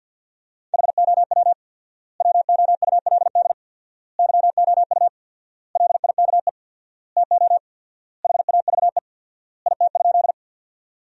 Morse code, heard as a keyed tone, about 52 wpm, 700 Hz. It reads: HOW WOULD YOU LIKE TO HAVE IT?
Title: 4x Word-Spacing - Sentences from Top 100 Words - 50wpm